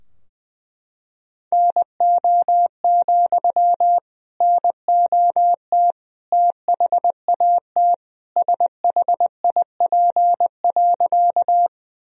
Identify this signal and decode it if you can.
Morse code, heard as a keyed tone, about 20 wpm, 700 Hz. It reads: NO, NOT THAT SHIP.
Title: Sentences from Top 1000 Words - Review Part 2 - 20wpm